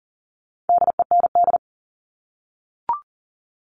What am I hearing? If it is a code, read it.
Morse code, heard as a keyed tone, about 40 wpm, 700 Hz. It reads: DEND
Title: All Cut Numbers - 4-Digit 40wpm